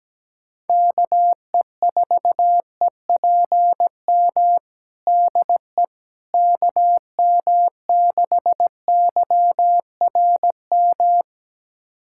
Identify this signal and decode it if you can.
Morse code, heard as a keyed tone, about 17 wpm, 700 Hz. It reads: KE4EPM DE KM6YRM
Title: USA Callsign DE Callsign - 6 Characters - 17wpm